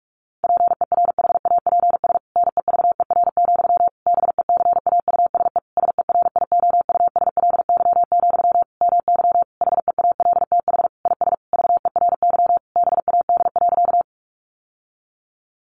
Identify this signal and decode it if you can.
Morse code, heard as a keyed tone, about 44 wpm, 700 Hz. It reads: PERHAPS NEVER, BECAUSE SERIOUSLY, MY HEALTH IS VERY BAD.
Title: Sentences from Top 1200 Words 45wpm